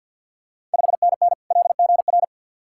Morse code, heard as a keyed tone, about 52 wpm, 700 Hz. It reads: 5NN LGR
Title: Ontario QSO Party Exchange 50wpm